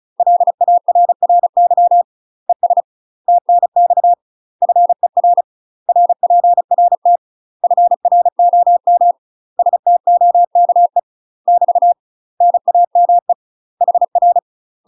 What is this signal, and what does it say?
Morse code, heard as a keyed tone, about 34 wpm, 700 Hz. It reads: LARRY ES TNX FER RPRT FROM STOKE <BT> NAME HR
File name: UK MASTER QSOs TRAINING/35WPM